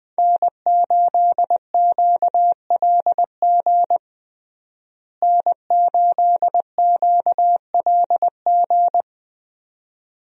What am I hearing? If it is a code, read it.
Morse code, heard as a keyed tone, about 20 wpm, 700 Hz. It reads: N8QLG N8QLG
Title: USA Call Signs - 5 Characters - Repeated 2x - 20wpm